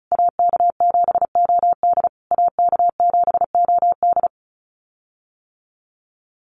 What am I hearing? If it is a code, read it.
Morse code, heard as a keyed tone, about 34 wpm, 700 Hz. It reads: AK7OD AK7OD
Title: USA Call Signs Repeated 3x 35wpm